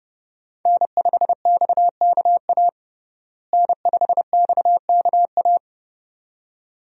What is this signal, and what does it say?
Morse code, heard as a keyed tone, about 30 wpm, 700 Hz. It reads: N5XKA N5XKA
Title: USA Call Signs - 5 Characters - Repeated 2x - 30wpm